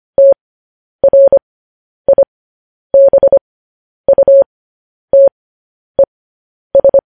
Morse code, heard as a keyed tone, about 25 wpm, spaced as for 10 wpm, 550 Hz. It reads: TRIBUTES
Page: News Headlines in Morse Code at 10 WPM